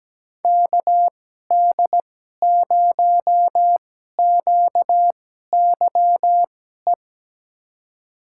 Morse code, heard as a keyed tone, about 17 wpm, spaced as for 12 wpm, 700 Hz. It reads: KD0QYE
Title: USA Call Signs - 6 Characters - 2x Character Spacing - 17wpm